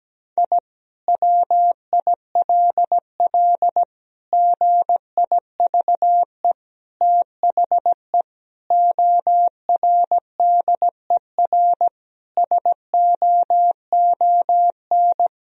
Morse code, heard as a keyed tone, about 17 wpm, 700 Hz. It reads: I WILL GIVE THE ORDER SOON
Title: Sentences from Top 1000 Words - Review Part 2 - 17wpm